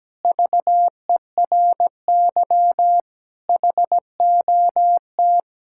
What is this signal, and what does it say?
Morse code, heard as a keyed tone, about 17 wpm, 700 Hz. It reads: VERY HOT
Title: Sentences from Top 1000 Words - Review Part 5 - 17wpm